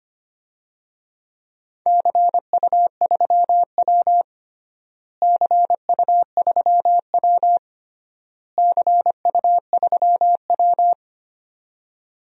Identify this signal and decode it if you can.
Morse code, heard as a keyed tone, about 25 wpm, 700 Hz. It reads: CU3W CU3W CU3W
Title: DXCC-100 Call Signs - 4 Characters - Repeated 3x - 25wpm